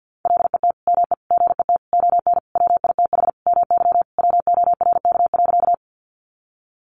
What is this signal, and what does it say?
Morse code, heard as a keyed tone, about 50 wpm, 700 Hz. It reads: LET ME GET ON WITH MY WORK.
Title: Sentences from Top 300 Words 50wpm